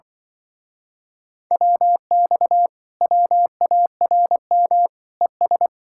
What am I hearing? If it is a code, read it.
Morse code, heard as a keyed tone, about 24 wpm, 700 Hz. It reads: WX WARM ES
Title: Speed-Racing Common QSO Phrases - 20wpm